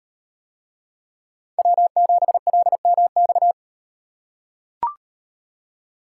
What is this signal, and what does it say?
Morse code, heard as a keyed tone, about 35 wpm, 700 Hz. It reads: W7LMX